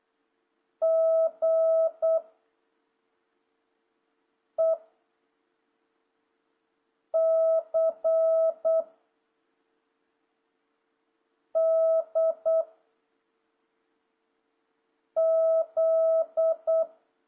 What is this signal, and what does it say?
Morse code, heard as a keyed tone, about 8 wpm, 650 Hz. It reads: G E C D Z
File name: jh8cbh / 2026cw